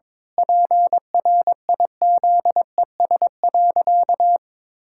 Morse code, heard as a keyed tone, about 22 wpm, 700 Hz. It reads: PRIZES.
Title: Pangram Challenge v2 - 22wpm